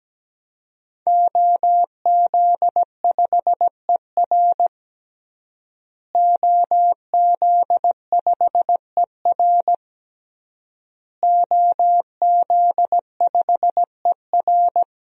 Morse code, heard as a keyed tone, about 17 wpm, 700 Hz. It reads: OZ5ER OZ5ER OZ5ER
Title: DXCC-100 Call Signs - 5 Characters - Repeated 3x - 17wpm